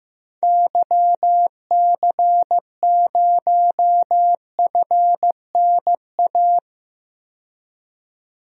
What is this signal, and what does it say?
Morse code, heard as a keyed tone, about 15 wpm, 700 Hz. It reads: YC0FNA
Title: DXCC-100 Call Signs - 6 Characters - 15wpm